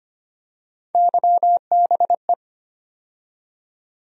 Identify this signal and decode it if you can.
Morse code, heard as a keyed tone, about 25 wpm, 700 Hz. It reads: YBE
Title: USA Call Sign Postfixes - 3 Letters 25wpm